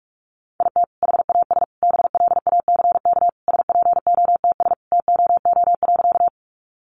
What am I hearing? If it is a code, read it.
Morse code, heard as a keyed tone, about 44 wpm, 700 Hz. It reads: IT HAS BLACK SPOTS TOO.
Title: Additional Easy Phrases 45wpm